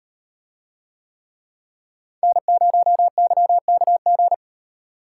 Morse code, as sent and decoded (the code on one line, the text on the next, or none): -. ----- -.-- -.- --.
N0YKG